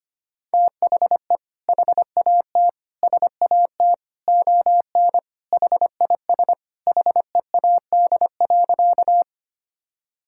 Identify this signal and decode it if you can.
Morse code, heard as a keyed tone, about 25 wpm, 700 Hz. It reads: THE HAT SAT ON HIS HEAD.
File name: Sentences with 2-Word Phrases - 25wpm